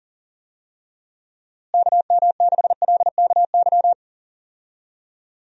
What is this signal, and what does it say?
Morse code, heard as a keyed tone, about 40 wpm, 700 Hz. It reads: KM6LKY
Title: USA Call Signs - 6 Characters - Repeated 2x - 40wpm